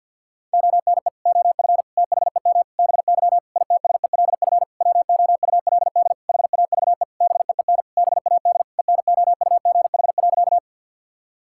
Morse code, heard as a keyed tone, about 49 wpm, 700 Hz. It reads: ONE OF THEM BY ITSELF WOULD HAVE BEEN BAD ENOUGH.
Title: Sentences from Top 1000 Words - Review Part 5 - 50wpm